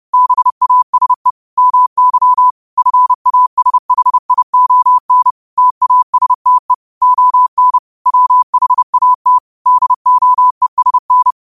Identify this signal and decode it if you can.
Morse code, heard as a keyed tone, about 30 wpm, 1000 Hz. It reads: BASE MY FASHION TASTE ON WHAT DOESN
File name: qotdfiles / 30wpm